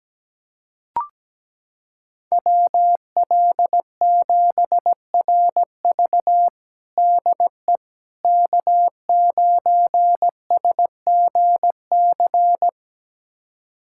Morse code, heard as a keyed tone, about 17 wpm, 700 Hz. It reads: WL7RV DE K9SGC
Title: USA Callsign DE Callsign - 5 Characters - 17wpm